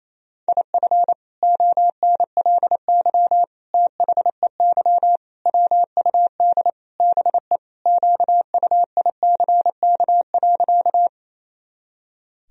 Morse code, heard as a keyed tone, about 28 wpm, 700 Hz. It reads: IF ONLY THEY WUD BE QUICK.
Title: Sentences from Top 500 Words and Abbreviations 28wpm